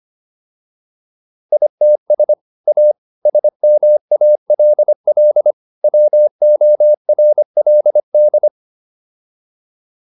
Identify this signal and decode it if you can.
Morse code, heard as a keyed tone, about 25 wpm, 600 Hz. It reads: ITS A SMALL WORLD